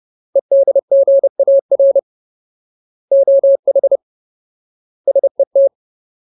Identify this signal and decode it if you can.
Morse code, heard as a keyed tone, about 30 wpm, 550 Hz. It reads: EDGAR OH SET
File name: Ultimate Word List / 30WPM Ultimate